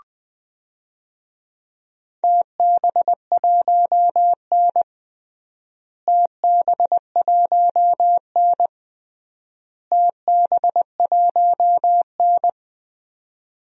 Morse code, heard as a keyed tone, about 20 wpm, 700 Hz. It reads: TB1N TB1N TB1N
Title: DXCC-100 Call Signs - 4 Characters - Repeated 3x - 20wpm